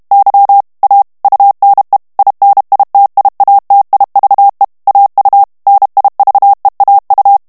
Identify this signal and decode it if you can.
Morse code, heard as a keyed tone, about 32 wpm, 800 Hz. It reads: Y A UNE INITIATIVE AU NIVEAU
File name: morse mp3 / FRANCAIS TEXTE CLAIR 1 VITESSE 30 MOTS MINUTES